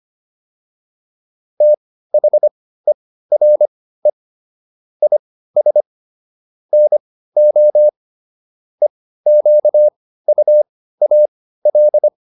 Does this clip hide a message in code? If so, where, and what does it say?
Yes: Morse code from 1.6 to 12.1 s, heard as a keyed tone, about 25 wpm, spaced as for 15 wpm, 600 Hz. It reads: THERE IS NO EQUAL